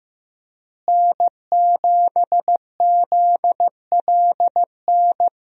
Morse code, heard as a keyed tone, about 15 wpm, 700 Hz. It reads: N7ZLN
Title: USA Call Signs - 5 Characters - Repeated 3x - 15wpm